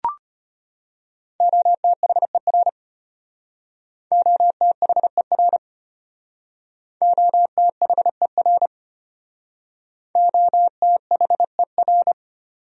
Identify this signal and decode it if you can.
Morse code, heard as a keyed tone, about 31 wpm, 700 Hz. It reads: OTHER OTHER OTHER OTHER